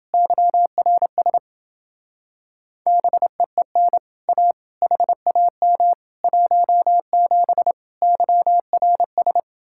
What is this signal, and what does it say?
Morse code, heard as a keyed tone, about 29 wpm, 700 Hz. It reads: YRS BEEN A HAM 17 YRS